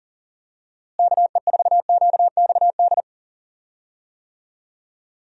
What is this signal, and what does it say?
Morse code, heard as a keyed tone, about 37 wpm, 700 Hz. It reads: KE4QXD
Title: USA Call Signs - 6 Characters - Repeated 3x - 40wpm